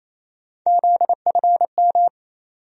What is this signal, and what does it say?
Morse code, heard as a keyed tone, about 27 wpm, 700 Hz. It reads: ZFM
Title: USA Call Sign Postfixes - 3 Letters 28wpm